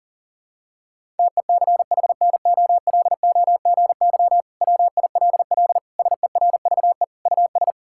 Morse code, heard as a keyed tone, about 37 wpm, 700 Hz. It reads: TECHNOLOGY WILL SERVE US
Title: Sentences from Top 1000 Words - Review Part 5 - 40wpm